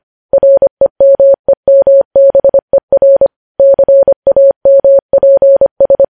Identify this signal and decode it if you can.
Morse code, heard as a keyed tone, about 25 wpm, 550 Hz. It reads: REMEMBER CAMPS
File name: podcast / News headlines-1x-25x25